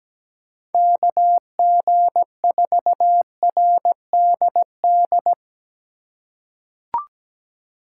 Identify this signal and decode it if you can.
Morse code, heard as a keyed tone, about 17 wpm, 700 Hz. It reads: KG4RDD
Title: USA Call Signs - 6 Characters - 2x Character Spacing - 17wpm